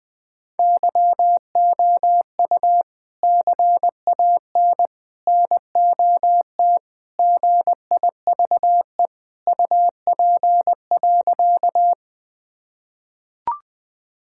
Morse code, heard as a keyed tone, about 20 wpm, 700 Hz. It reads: YOU CAN NOT GIVE UP.
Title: Sentences with 3-Word Phrases - 20wpm